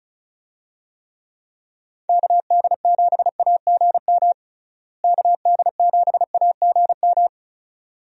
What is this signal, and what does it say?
Morse code, heard as a keyed tone, about 36 wpm, 700 Hz. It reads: KD7AGM KD7AGM
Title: USA Call Signs - 6 Characters - Repeated 2x - 35wpm